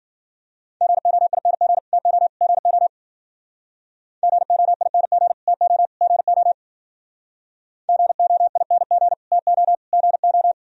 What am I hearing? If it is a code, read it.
Morse code, heard as a keyed tone, about 50 wpm, 700 Hz. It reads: GOING TO GO GOING TO GO GOING TO GO